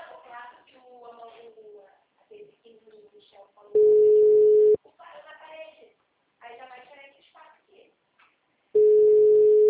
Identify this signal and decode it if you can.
Ringback tone (Europe, 425 Hz), with speech.
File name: out- -2038- - - .39810